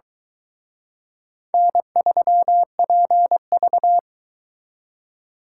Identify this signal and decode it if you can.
Morse code, heard as a keyed tone, about 23 wpm, 700 Hz. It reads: N3PV